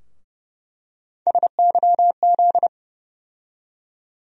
Morse code, heard as a keyed tone, about 31 wpm, 700 Hz. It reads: SYZ
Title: USA Call Sign Postfixes - 3 Letters 30wpm